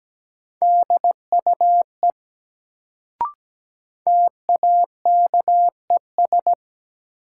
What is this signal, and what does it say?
Morse code, heard as a keyed tone, about 17 wpm, 700 Hz. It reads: DUE TAKES
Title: Sets of 5 Words Encore - New Word Drill - 17wpm